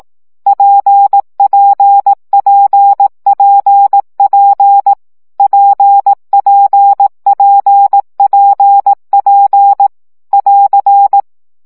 Morse code, heard as a keyed tone, about 18 wpm, 800 Hz. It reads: PPPPP PPPPP <AR>
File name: CW gyakorlas p 15 baud